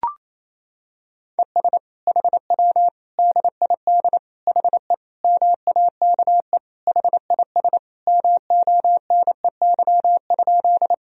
Morse code, heard as a keyed tone, about 28 wpm, 700 Hz. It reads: ES HW DID HE MAKE HIS MONEY?
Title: Sentences from Top 400 Words and Abbreviations 28wpm